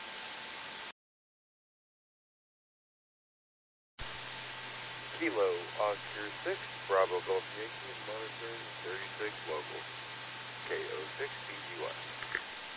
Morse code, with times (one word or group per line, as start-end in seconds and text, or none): none